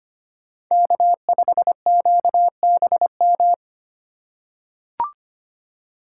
Morse code, heard as a keyed tone, about 25 wpm, 700 Hz. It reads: K5QBM
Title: USA Call Signs - 5 Characters - 2x Character Spacing - 25wpm